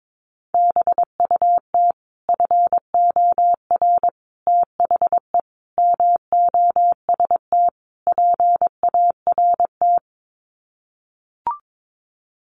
Morse code, heard as a keyed tone, about 22 wpm, 700 Hz. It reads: BUT FOR THE MOST PART
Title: Sets of 5 Words Encore 22wpm